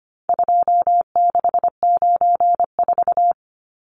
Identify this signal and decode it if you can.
Morse code, heard as a keyed tone, about 25 wpm, 700 Hz. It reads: 2694
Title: Numbers - 4-Digit 25wpm